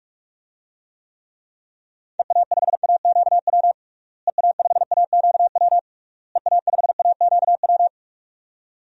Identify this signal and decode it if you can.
Morse code, heard as a keyed tone, about 46 wpm, 700 Hz. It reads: EA5AQW EA5AQW EA5AQW
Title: DXCC-100 Call Signs - 6 Characters - Repeated 3x - 45wpm